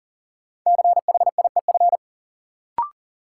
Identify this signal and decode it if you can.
Morse code, heard as a keyed tone, about 40 wpm, 700 Hz. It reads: CHIEF
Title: Sets of 5 Words Encore - New Word Drill - 40wpm